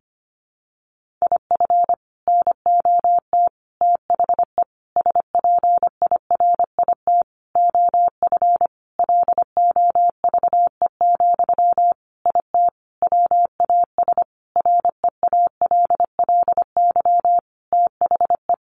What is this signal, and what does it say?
Morse code, heard as a keyed tone, about 25 wpm, 700 Hz. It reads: IF NOT THE SPIRIT OF LOVE, IT WAS REALLY THE
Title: Sentences from Top 1400 Words 25wpm